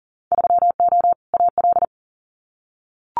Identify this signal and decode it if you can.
Morse code, heard as a keyed tone, about 40 wpm, 700 Hz. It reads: 3O AL
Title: Winter Field Day Exchanges 40wpm